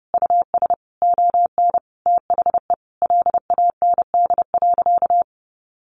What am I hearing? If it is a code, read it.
Morse code, heard as a keyed tone, about 30 wpm, 700 Hz. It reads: US ON THE LAND.
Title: Sentences from Top 1200 Words 30wpm